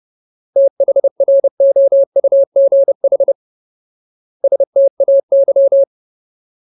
Morse code, heard as a keyed tone, about 30 wpm, 550 Hz. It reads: THROUGH STAY
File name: Ultimate Word List / 30WPM Ultimate